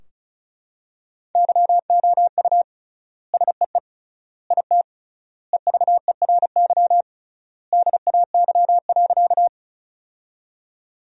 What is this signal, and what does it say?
Morse code, heard as a keyed tone, about 34 wpm, 700 Hz. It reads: YOU SEE IT EVERY DAY.
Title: 3x Word-Spacing - Sentences from Top 200 Words - 35wpm